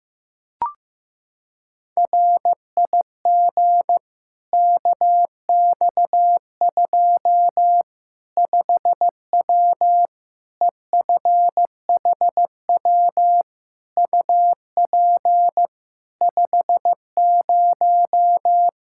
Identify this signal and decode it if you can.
Morse code, heard as a keyed tone, about 15 wpm, 700 Hz. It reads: RIG KX2 5W EFHW UP 50
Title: Practice QSO - Exchange 2 - Info - 15wpm